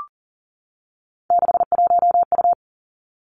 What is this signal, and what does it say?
Morse code, heard as a keyed tone, about 39 wpm, 700 Hz. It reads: 61U
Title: Three Challenging Letter-Number Combinations 40wpm